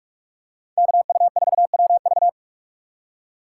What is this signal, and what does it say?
Morse code, heard as a keyed tone, about 46 wpm, 700 Hz. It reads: KU4WV
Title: USA Call Signs - 5 Characters - 3x Character Spacing - 45wpm